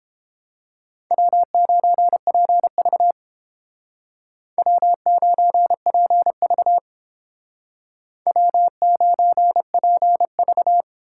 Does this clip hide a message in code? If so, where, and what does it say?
Yes: Morse code from 1.1 to 10.8 s, heard as a keyed tone, about 30 wpm, 700 Hz. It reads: W9PV W9PV W9PV